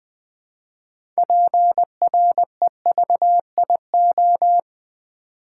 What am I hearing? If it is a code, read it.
Morse code, heard as a keyed tone, about 20 wpm, 700 Hz. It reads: PREVIO
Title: Build A Word - 3 Letters At A Time - 8-Letter Words - 20wpm